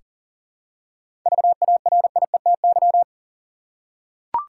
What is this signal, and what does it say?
Morse code, heard as a keyed tone, about 40 wpm, 700 Hz. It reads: VARIETY
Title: Seven Letter Words - 40wpm